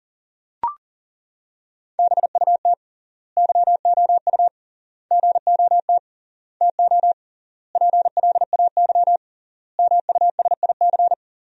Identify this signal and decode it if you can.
Morse code, heard as a keyed tone, about 40 wpm, 700 Hz. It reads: BUT YOU GOT TO PLAY MUSIC
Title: 3x Word-Spacing - Sentences from Top 300 Words - 40wpm